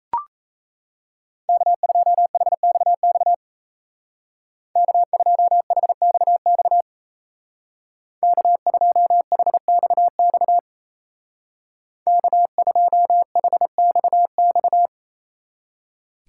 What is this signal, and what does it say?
Morse code, heard as a keyed tone, about 35 wpm, 700 Hz. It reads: K2HXX K2HXX K2HXX K2HXX
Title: Speed Racing - USA 1x3 Call Signs 28wpm